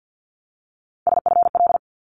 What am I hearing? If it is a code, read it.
Morse code, heard as a keyed tone, about 50 wpm, 700 Hz. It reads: SFL
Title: ARRL Sections 50wpm